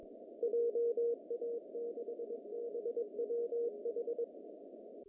Morse code, heard as a keyed tone, about 22 wpm, 470 Hz. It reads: JA6BWH
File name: CW / cqww-cw-k5tr